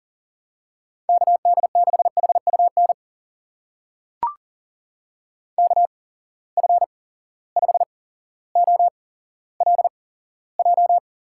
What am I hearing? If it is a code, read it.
Morse code, heard as a keyed tone, about 40 wpm, spaced as for 17 wpm, 700 Hz. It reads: KD6HUN KF5OLJ
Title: USA Call Signs - 6 Characters - 8x Character Spacing - 40wpm